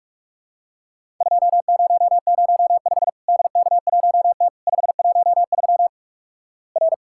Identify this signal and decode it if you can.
Morse code, heard as a keyed tone, about 47 wpm, 700 Hz. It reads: 2005 DK1T 513 R
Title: Worked All Europe DX Contest - QTC Exchange - 45wpm